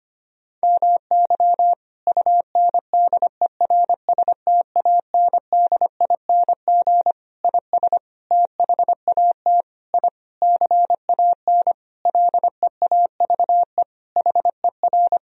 Morse code, heard as a keyed tone, about 25 wpm, 700 Hz. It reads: MY UNDERSTANDING IS THAT I CAN LEAVE HER
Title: Sentences from Top 1000 Words - Review Part 3 - 25wpm